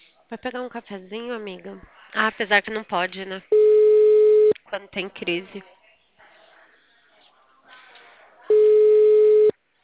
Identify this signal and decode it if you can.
Ringback tone (Europe, 425 Hz), with speech.
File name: out- -2009- - - .34723